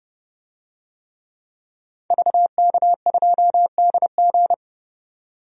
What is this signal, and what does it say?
Morse code, heard as a keyed tone, about 29 wpm, 700 Hz. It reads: VK2DG
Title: DXCC-100 Call Signs - 5 Characters - 30wpm